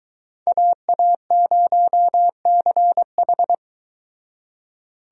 Morse code, heard as a keyed tone, about 23 wpm, 700 Hz. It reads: AA0CH